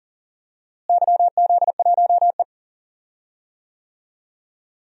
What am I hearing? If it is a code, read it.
Morse code, heard as a keyed tone, about 37 wpm, 700 Hz. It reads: YZ1E
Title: DXCC-100 Call Signs - 4 Characters - Repeated 2x - 40wpm